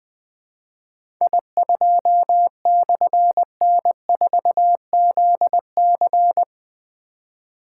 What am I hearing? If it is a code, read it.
Morse code, heard as a keyed tone, about 20 wpm, 700 Hz. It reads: I2/N4ZC
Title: Difficult Call Signs 20wpm